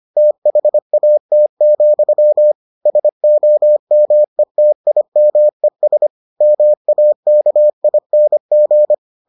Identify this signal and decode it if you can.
Morse code, heard as a keyed tone, about 25 wpm, 600 Hz. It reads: THAT, SOMETIMES MAKING